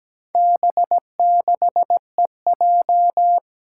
Morse code, heard as a keyed tone, about 17 wpm, 700 Hz. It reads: B6EJ
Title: Four Challenging Letter-Number Combinations 17wpm